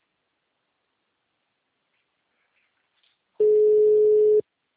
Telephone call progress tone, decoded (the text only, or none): ringback tone